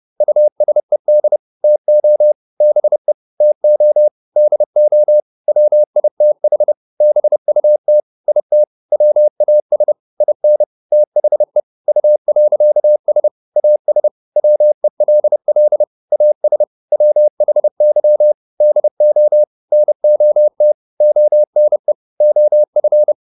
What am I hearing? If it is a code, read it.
Morse code, heard as a keyed tone, about 30 wpm, 600 Hz. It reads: USED TO BE TO DO WITH BUT IT WAS IN THE U.S AS WELL AS WHY DO NOT ONE OF